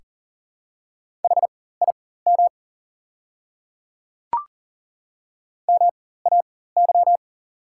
Morse code, heard as a keyed tone, about 40 wpm, spaced as for 19 wpm, 690 Hz. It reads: HIM MAY
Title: 4x Character Spacing - Top 100 Words - 40wpm